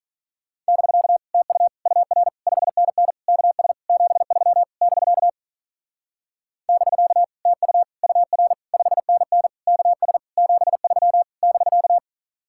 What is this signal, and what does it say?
Morse code, heard as a keyed tone, about 44 wpm, 700 Hz. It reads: <BK> TU UR 5NN KS 73 <BK> <BK> TU UR 5NN KS 73 <BK>
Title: Speed-Racing Common QSO Phrases - 35wpm